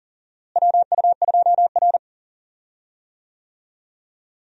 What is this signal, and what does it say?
Morse code, heard as a keyed tone, about 39 wpm, 700 Hz. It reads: WU2R
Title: USA Call Signs - 4 Characters - Repeated 2x - 40wpm